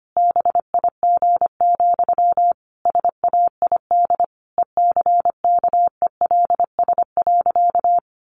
Morse code, heard as a keyed tone, about 25 wpm, 700 Hz. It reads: BIG, SAID ECKELS.
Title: A Sound Of Thunder 25wpm